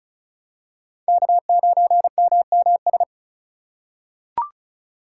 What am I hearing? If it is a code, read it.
Morse code, heard as a keyed tone, about 34 wpm, 700 Hz. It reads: K9MMS
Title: K1USN SST QSO Practice - 35wpm